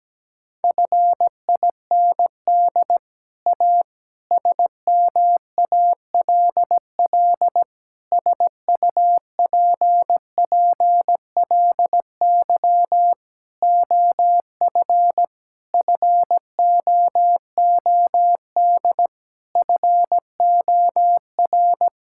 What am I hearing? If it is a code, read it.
Morse code, heard as a keyed tone, about 17 wpm, 700 Hz. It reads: FIND A SMALL SUPPLY OF FOOD FOR
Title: Sentences from Top 1000 Words - Review Part 5 - 17wpm